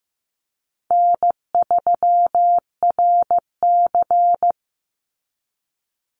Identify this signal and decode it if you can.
Morse code, heard as a keyed tone, about 15 wpm, 700 Hz. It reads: N3RC
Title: Call Signs Repeated 2x - 15wpm